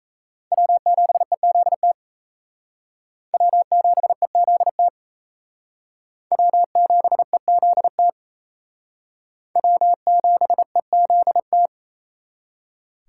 Morse code, heard as a keyed tone, about 36 wpm, 700 Hz. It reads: W7EZT W7EZT W7EZT W7EZT